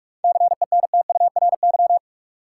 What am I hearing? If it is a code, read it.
Morse code, heard as a keyed tone, about 46 wpm, 700 Hz. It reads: CENTURY
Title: Sets of 5 Words - New Word Drill - 45wpm